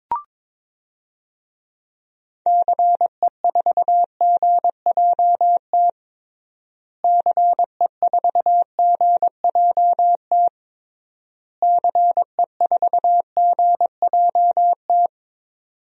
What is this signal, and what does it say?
Morse code, heard as a keyed tone, about 22 wpm, 700 Hz. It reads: CE4GJT CE4GJT CE4GJT
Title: DXCC-100 Call Signs - 6 Characters - Repeated 3x - 22wpm